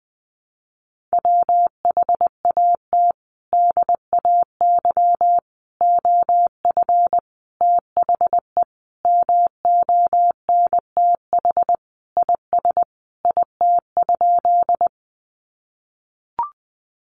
Morse code, heard as a keyed tone, about 20 wpm, 700 Hz. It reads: WHAT DAY OF THE MONTH IS IT?
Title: Sentences from Top 800 Words 20wpm